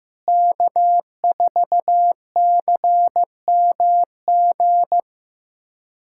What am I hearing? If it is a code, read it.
Morse code, heard as a keyed tone, about 15 wpm, 700 Hz. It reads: K4CMG
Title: USA Callsign DE Callsign - 5 Characters - 15wpm